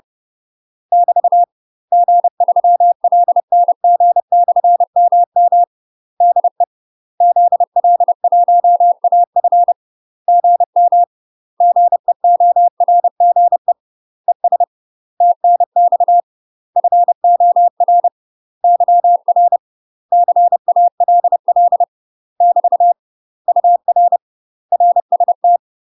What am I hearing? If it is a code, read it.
Morse code, heard as a keyed tone, about 30 wpm, 700 Hz. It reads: <BT> G3LNG/MM DE ZL1AF GM GEORGE ES TNX FOR YR CALL <BT> UR RST
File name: UK MASTER QSOs TRAINING / 30WPM / UK MASTER QSOs 0021